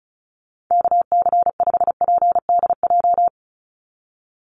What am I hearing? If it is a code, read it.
Morse code, heard as a keyed tone, about 34 wpm, 700 Hz. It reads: KC5PDJ
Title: USA Call Signs 35wpm